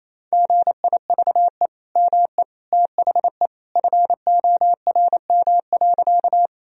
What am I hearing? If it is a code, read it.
Morse code, heard as a keyed tone, about 28 wpm, 700 Hz. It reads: GIVE ME THE FORM.
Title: Sentences from Top 1000 Words - Review Part 1 - 28wpm